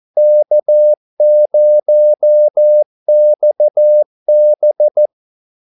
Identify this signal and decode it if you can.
Morse code, heard as a keyed tone, about 14 wpm, 600 Hz. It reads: K0XB
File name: Callsigns-14wpm